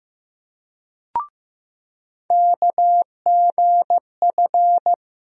Morse code, heard as a keyed tone, about 15 wpm, 700 Hz. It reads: KGF
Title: USA Call Sign Postfixes - 3 Letters 15wpm